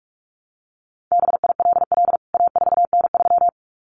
Morse code, heard as a keyed tone, about 44 wpm, 700 Hz. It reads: BILL A4N3
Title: CWT Exchange 45wpm